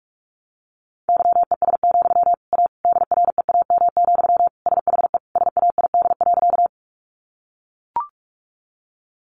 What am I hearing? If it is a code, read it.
Morse code, heard as a keyed tone, about 44 wpm, 700 Hz. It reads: YES, A DREAM, SHE SAID.
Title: Sentences from Top 1500 Words 45wpm